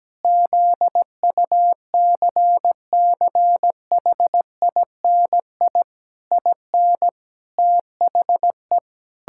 Morse code, heard as a keyed tone, about 17 wpm, 700 Hz. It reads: ZUCCHINI IN THE
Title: Pangram Challenge v2 - 17wpm